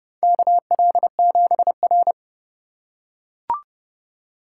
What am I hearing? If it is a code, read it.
Morse code, heard as a keyed tone, about 29 wpm, 700 Hz. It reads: KL7R
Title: USA Call Signs - 4 Characters - 4x Character Spacing - 30wpm